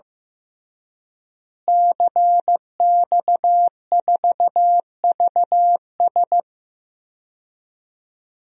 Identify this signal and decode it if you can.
Morse code, heard as a keyed tone, about 15 wpm, 700 Hz. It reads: CX4VS
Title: DXCC-100 Call Signs - 5 Characters - Repeated 3x - 15wpm